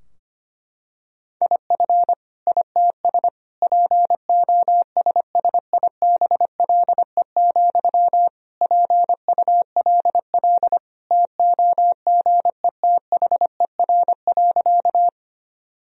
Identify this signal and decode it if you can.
Morse code, heard as a keyed tone, about 25 wpm, 700 Hz. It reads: IF ITS POSSIBLE, PULL TOGETHER.
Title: Sentences from Top 1000 Words - Review Part 2 - 25wpm